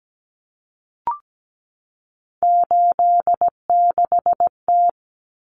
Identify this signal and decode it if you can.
Morse code, heard as a keyed tone, about 17 wpm, 700 Hz. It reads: 86T
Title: Common Cut Numbers - 3-Digit 17wpm